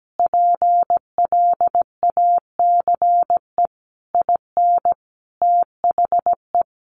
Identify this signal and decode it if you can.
Morse code, heard as a keyed tone, about 17 wpm, 700 Hz. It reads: PLACE IN THE
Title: Sentences from Top 1300 Words 17wpm